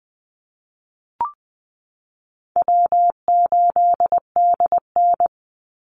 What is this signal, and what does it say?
Morse code, heard as a keyed tone, about 20 wpm, 700 Hz. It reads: W8DN
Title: Call Signs Repeated 2x - 20wpm